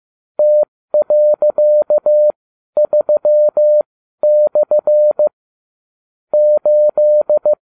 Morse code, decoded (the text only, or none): T.3/ 8